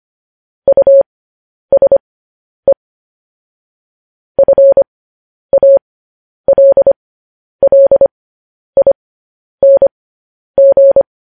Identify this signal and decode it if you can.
Morse code, heard as a keyed tone, about 25 wpm, spaced as for 10 wpm, 550 Hz. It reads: USE FALLING
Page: news-headlines-in-morse-code-at-10-wpm